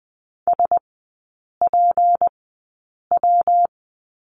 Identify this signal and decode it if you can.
Morse code, heard as a keyed tone, about 20 wpm, spaced as for 8 wpm, 690 Hz. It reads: SPW
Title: 205 Lesson 33e - Letter j - Three Characters 20wpm